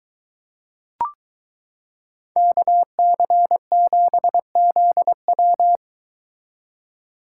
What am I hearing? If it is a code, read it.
Morse code, heard as a keyed tone, about 22 wpm, 700 Hz. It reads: KC7ZW